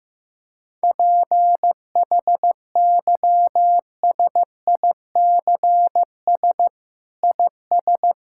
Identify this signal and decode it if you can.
Morse code, heard as a keyed tone, about 15 wpm, 700 Hz. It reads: PHYSICS IS
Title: Sentences from Top 1000 Words - Review Part 5 - 15wpm